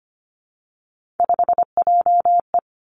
Morse code, heard as a keyed tone, about 25 wpm, 700 Hz. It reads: 5JE
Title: Three Challenging Letter-Number Combinations 25wpm